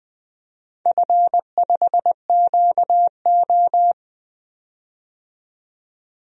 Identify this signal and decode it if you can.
Morse code, heard as a keyed tone, about 20 wpm, 700 Hz. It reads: F5QO
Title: DXCC-100 Call Signs - 4 Characters - 20wpm